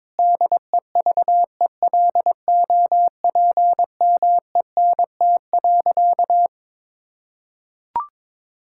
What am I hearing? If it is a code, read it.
Morse code, heard as a keyed tone, about 22 wpm, 700 Hz. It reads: DEVELOPMENT.
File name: Difficult Sentences with Hints - 22wpm